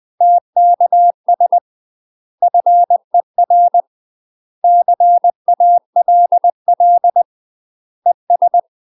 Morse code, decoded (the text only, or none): TKS FER CALL ES